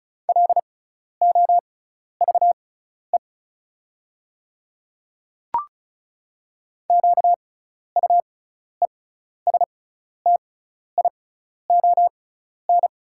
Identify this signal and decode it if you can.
Morse code, heard as a keyed tone, about 36 wpm, spaced as for 12 wpm, 700 Hz. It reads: LOVE QUESTION
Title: 6x Character Spacing - Top 500 Words - 35wpm